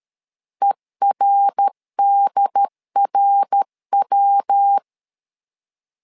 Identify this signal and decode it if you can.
Morse code, heard as a keyed tone, about 13 wpm, 800 Hz. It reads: ERDRW
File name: LEZIONI / 3 MWRO / 3 MWR 12wpm 1711